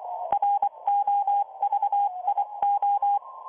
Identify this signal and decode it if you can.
Morse code, heard as a keyed tone, about 24 wpm, 810 Hz, with radio noise. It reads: ROVIO